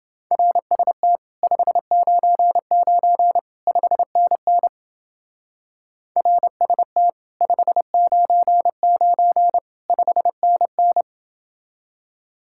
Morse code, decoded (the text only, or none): RST 599 5NN RST 599 5NN